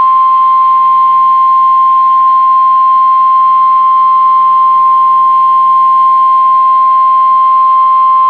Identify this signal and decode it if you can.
Emergency Alert System (SAME) tones (the 1050 Hz attention tone). It is heard from 1.0 to 7.7 s.